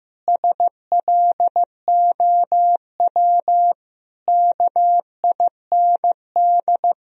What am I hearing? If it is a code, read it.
Morse code, heard as a keyed tone, about 16 wpm, 700 Hz. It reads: SLOW KIND
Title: Sentences from Top 1000 Words - Review Part 2 - 15wpm